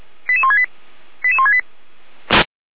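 Five-tone selective call (ZVEI), heard as quick runs of five tones, tones of about 70 ms. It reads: 88168 88168
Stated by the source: ZVEI five-tone call 88247 received on an SDR, from the feuerwehr-orlamuende.org fire brigade audio server.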